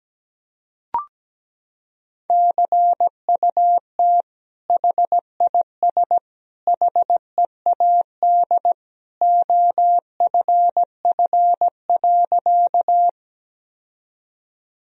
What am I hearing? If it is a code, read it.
Morse code, heard as a keyed tone, about 17 wpm, 700 Hz. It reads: CUT HIS HEAD OFF.
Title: Sentences from Top 1000 Words - Review Part 1 - 17wpm